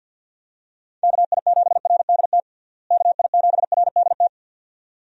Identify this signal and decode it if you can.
Morse code, heard as a keyed tone, about 51 wpm, 700 Hz. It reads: KI7RDT KI7RDT
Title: USA Call Signs - 6 Characters - Repeated 2x - 50wpm